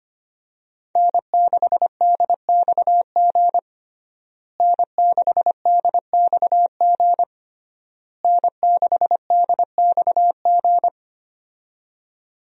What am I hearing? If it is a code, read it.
Morse code, heard as a keyed tone, about 25 wpm, 700 Hz. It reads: N6DXG N6DXG N6DXG
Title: USA Call Signs - 5 Characters - Repeated 3x - 25wpm